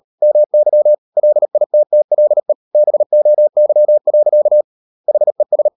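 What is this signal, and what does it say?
Morse code, heard as a keyed tone, about 38 wpm, 600 Hz. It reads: MY LITTLE BOY. HES
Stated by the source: Morse code practice at 38 WPM, sending Ken Follett's novel Never in English, from CW med Gnister.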